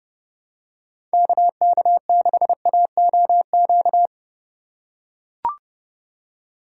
Morse code, heard as a keyed tone, about 29 wpm, 700 Hz. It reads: KK6AOQ